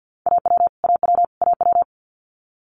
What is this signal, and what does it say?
Morse code, heard as a keyed tone, about 50 wpm, 700 Hz. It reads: AW AW AW
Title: Multi-letter Phonemes Repeated 3x 50wpm